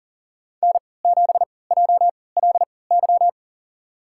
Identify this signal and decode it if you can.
Morse code, heard as a keyed tone, about 39 wpm, spaced as for 22 wpm, 700 Hz. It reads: N7JLY
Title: USA Call Signs - 3x Character Spacing - 40wpm